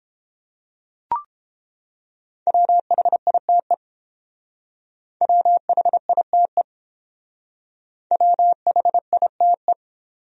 Morse code, heard as a keyed tone, about 28 wpm, 700 Hz. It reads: WHITE WHITE WHITE